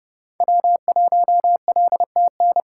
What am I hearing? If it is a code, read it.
Morse code, heard as a keyed tone, about 29 wpm, 700 Hz. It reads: W1LTN